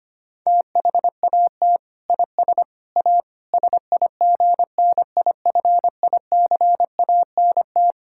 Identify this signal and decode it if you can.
Morse code, heard as a keyed tone, about 25 wpm, 700 Hz. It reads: THAT IS A SIGNIFICANT
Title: Sentences from Top 1000 Words - Review Part 6 - 25wpm